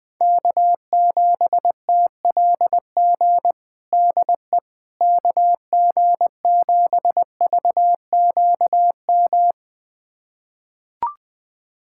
Morse code, heard as a keyed tone, about 20 wpm, 700 Hz. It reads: K7TLG DE KG7VQM
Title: USA Callsign DE Callsign 20wpm